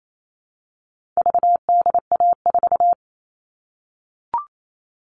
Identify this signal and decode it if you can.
Morse code, heard as a keyed tone, about 28 wpm, 700 Hz. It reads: VDA4
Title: All Cut Numbers - 4-Digit 28wpm